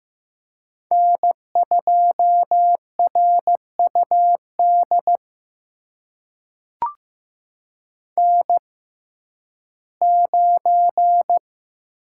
Morse code, heard as a keyed tone, about 15 wpm, spaced as for 11 wpm, 700 Hz. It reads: N2RUD N9
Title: USA Call Signs - 5 Characters - 6x Character Spacing - 15wpm